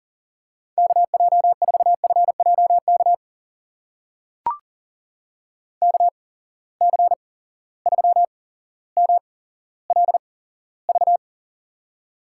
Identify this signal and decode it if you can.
Morse code, heard as a keyed tone, about 40 wpm, spaced as for 17 wpm, 700 Hz. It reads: KJ4FJK KC3MLV
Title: USA Call Signs - 6 Characters - 8x Character Spacing - 40wpm